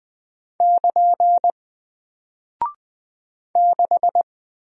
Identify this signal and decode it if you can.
Morse code, heard as a keyed tone, about 20 wpm, 700 Hz. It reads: <KN> 6
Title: Single Character (Letter:Number:Punctuation:Prosign) Alt 20wpm